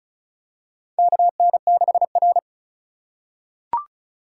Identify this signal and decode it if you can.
Morse code, heard as a keyed tone, about 36 wpm, 700 Hz. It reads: KN6R
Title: USA Call Signs - 4 Characters - 2x Character Spacing - 35wpm